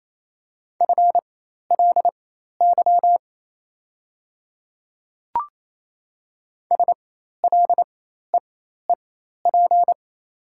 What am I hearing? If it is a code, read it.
Morse code, heard as a keyed tone, about 28 wpm, spaced as for 13 wpm, 700 Hz. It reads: FLY SLEEP
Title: 4x Character Spacing - Top 500 Words - 28wpm